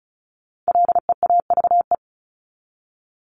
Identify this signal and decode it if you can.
Morse code, heard as a keyed tone, about 34 wpm, 700 Hz. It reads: LEAVE
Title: ING Suffix - Top 100 Words with Base Words 35wpm